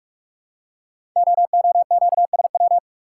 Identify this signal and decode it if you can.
Morse code, heard as a keyed tone, about 44 wpm, 700 Hz. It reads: OOQSW
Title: Code Groups - 5 Letters - 45wpm